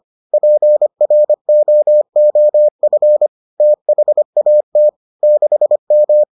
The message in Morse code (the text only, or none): PROOF THAT 6M